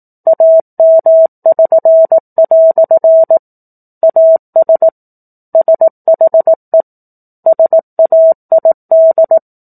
Morse code, heard as a keyed tone, about 18 wpm, 650 Hz. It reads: AM<SN>" AS SHE SAID
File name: 04-Chapter-04 18 WPM